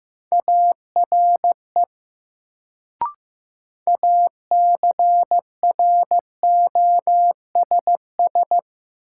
Morse code, heard as a keyed tone, about 15 wpm, 700 Hz. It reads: ARE ACROSS
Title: Sets of 2 Words - New Word Drill - 15wpm